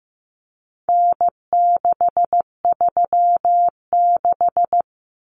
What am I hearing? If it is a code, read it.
Morse code, heard as a keyed tone, about 15 wpm, 700 Hz. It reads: N636
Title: Common Cut Numbers - 4-Digit 15wpm